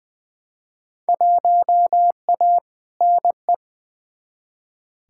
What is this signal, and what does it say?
Morse code, heard as a keyed tone, about 21 wpm, 700 Hz. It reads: 1A NE
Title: Field Day Exchanges Encore - 20wpm